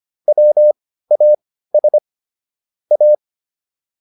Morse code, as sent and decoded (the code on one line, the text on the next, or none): .-- .- ... / .-
WAS A